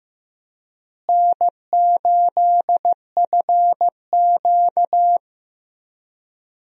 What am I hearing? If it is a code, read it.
Morse code, heard as a keyed tone, about 15 wpm, 700 Hz. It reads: N8FQ
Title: USA Call Signs - 4 Characters - Repeated 3x - 15wpm